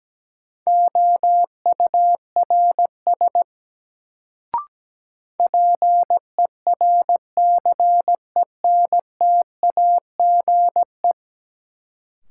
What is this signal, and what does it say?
Morse code, heard as a keyed tone, about 17 wpm, 700 Hz. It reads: OURS PERCENTAGE
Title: Sentences from Top 1300 Words - New Word Drill Alt - 17wpm